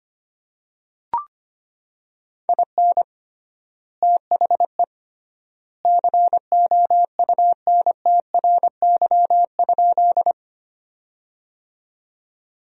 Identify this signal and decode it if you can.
Morse code, heard as a keyed tone, about 25 wpm, 700 Hz. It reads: IN THE COUNTRY?
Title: 3x Word-Spacing - Sentences from Top 300 Words - 25wpm